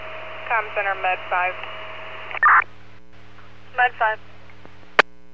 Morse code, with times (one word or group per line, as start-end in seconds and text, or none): none